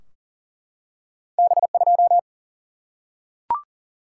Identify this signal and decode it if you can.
Morse code, heard as a keyed tone, about 40 wpm, 700 Hz. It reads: B2
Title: Single-Letter Single-Number 40wpm